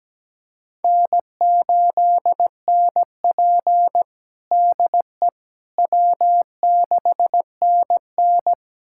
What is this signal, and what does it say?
Morse code, heard as a keyed tone, about 17 wpm, 700 Hz. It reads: N8NP DE W6NN
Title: USA Callsign DE Callsign - 4 Characters - 17wpm